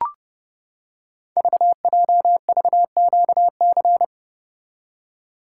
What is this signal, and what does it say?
Morse code, heard as a keyed tone, about 30 wpm, 700 Hz. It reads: VJVQC
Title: Code Groups - 5 Letters - 30wpm